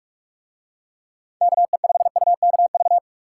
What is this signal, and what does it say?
Morse code, heard as a keyed tone, about 46 wpm, 700 Hz. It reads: KE5UKV